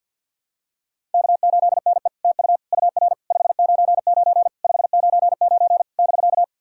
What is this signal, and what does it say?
Morse code, heard as a keyed tone, about 50 wpm, 700 Hz. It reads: K8NE TU UR 599 599 <BK>
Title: SOTA QSO Practice 50wpm